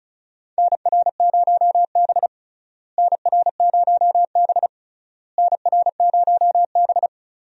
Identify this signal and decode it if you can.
Morse code, heard as a keyed tone, about 36 wpm, 700 Hz. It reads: NR0B NR0B NR0B
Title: USA Call Signs - 4 Characters - Repeated 3x - 35wpm